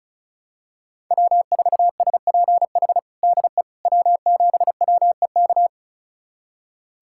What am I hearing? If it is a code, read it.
Morse code, heard as a keyed tone, about 36 wpm, 700 Hz. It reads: W4SPH DE W7WEK
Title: USA Callsign DE Callsign - 5 Characters - 35wpm